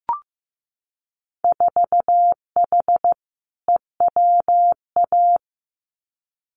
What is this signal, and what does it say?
Morse code, heard as a keyed tone, about 15 wpm, 700 Hz. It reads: 4H EWA
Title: Winter Field Day Exchanges Encore 15wpm